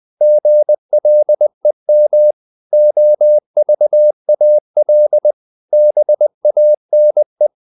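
Morse code, heard as a keyed tone, about 20 wpm, 600 Hz. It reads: GLEM OVAL BANE
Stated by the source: Morse code practice of short Danish words at 20 wpm, from CW med Gnister.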